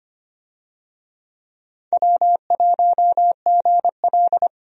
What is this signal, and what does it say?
Morse code, heard as a keyed tone, about 25 wpm, 700 Hz. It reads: W1GL
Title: USA Call Signs - 4 Characters - 25wpm